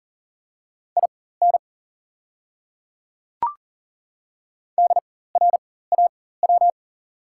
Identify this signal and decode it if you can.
Morse code, heard as a keyed tone, about 40 wpm, spaced as for 19 wpm, 700 Hz. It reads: IN DRAW
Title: 4x Character Spacing - Top 300 Words - 40wpm